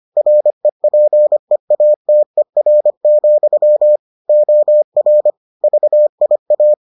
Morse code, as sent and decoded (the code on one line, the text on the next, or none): .-. .--. .- - . .-. --..-- / --- .-. / ...- .. .-
REPEATER, OR VIA